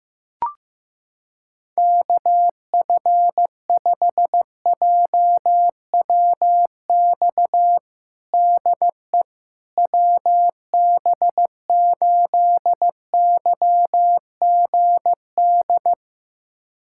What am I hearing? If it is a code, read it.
Morse code, heard as a keyed tone, about 15 wpm, 700 Hz. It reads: KF5JWX DE WB8YGD
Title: USA Callsign DE Callsign 15wpm